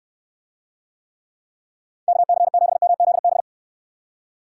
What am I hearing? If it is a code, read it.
Morse code, heard as a keyed tone, about 68 wpm, 700 Hz. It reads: KC7MZB